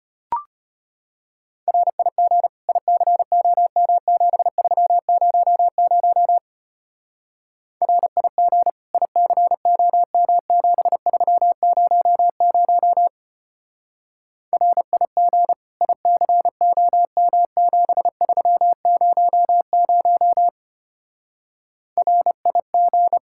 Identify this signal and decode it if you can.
Morse code, heard as a keyed tone, about 32 wpm, 700 Hz. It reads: RIG ICOM7300 RIG ICOM7300 RIG ICOM7300 RIG